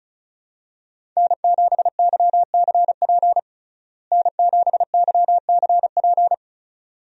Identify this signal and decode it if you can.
Morse code, heard as a keyed tone, about 34 wpm, 700 Hz. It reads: N7YCP N7YCP
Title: USA Call Signs - 5 Characters - Repeated 2x - 35wpm